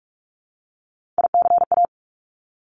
Morse code, heard as a keyed tone, about 46 wpm, 700 Hz. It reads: ICA
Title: Top Three Letter Combinations in Words 45wpm